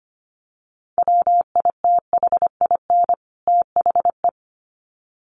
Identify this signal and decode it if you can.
Morse code, heard as a keyed tone, about 25 wpm, 700 Hz. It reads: WITHIN THE
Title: Sets of 2 Words 25wpm